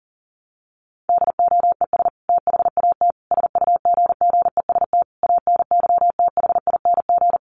Morse code, heard as a keyed tone, about 37 wpm, 700 Hz. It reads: DOES THAT SUGGEST ANYTHING
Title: Sentences from Top 1300 Words 40wpm